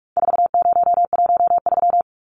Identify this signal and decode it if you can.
Morse code, heard as a keyed tone, about 44 wpm, 700 Hz. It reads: 4013
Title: Numbers - 4-Digit 45wpm